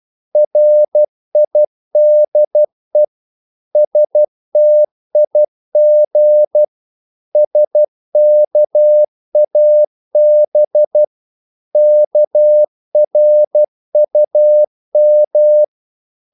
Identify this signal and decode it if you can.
Morse code, heard as a keyed tone, about 12 wpm, 600 Hz. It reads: RIDE STIG SKAB KRUM
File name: Korte ord DK 12wpm